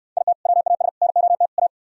Morse code, heard as a keyed tone, about 67 wpm, 700 Hz. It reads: IT WAS NOT A